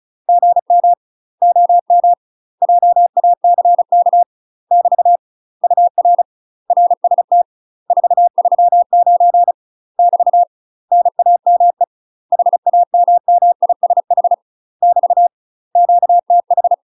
Morse code, heard as a keyed tone, about 36 wpm, 700 Hz. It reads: GM OM JACK <BT> UR RST 439 <BT> NAME HAMMISH <BT> QTH
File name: UK MASTER QSOs TRAINING / 35WPM / UK MASTER QSOs 0032